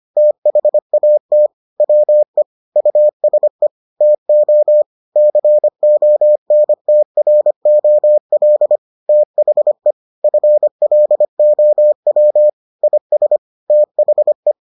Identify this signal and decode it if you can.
Morse code, heard as a keyed tone, about 25 wpm, 600 Hz. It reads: THAT WE USE TO CONTROL THE FLOW IS THE